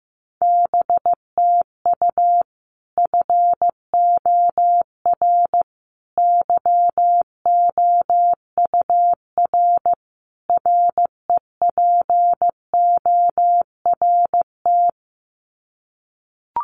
Morse code, heard as a keyed tone, about 15 wpm, 700 Hz. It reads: BTU FOR YOUR REPORT
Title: Common QSO Phrases 15wpm